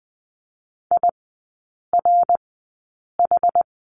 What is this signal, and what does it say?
Morse code, heard as a keyed tone, about 20 wpm, spaced as for 8 wpm, 700 Hz. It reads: IRH
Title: 067 Lesson 12e - Letter d - Three Characters 20wpm